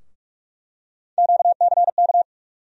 Morse code, heard as a keyed tone, about 46 wpm, 700 Hz. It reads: QCK
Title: USA Call Sign Postfixes - 3 Letters 45wpm